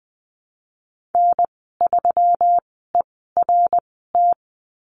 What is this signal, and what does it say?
Morse code, heard as a keyed tone, about 20 wpm, spaced as for 15 wpm, 700 Hz. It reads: N3ERT
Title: 128 Lesson 21g - Number 6 - Call Signs 20wpm